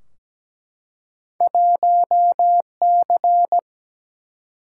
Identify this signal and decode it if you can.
Morse code, heard as a keyed tone, about 17 wpm, 700 Hz. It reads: 1C
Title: Single-Letter Single-Number 17wpm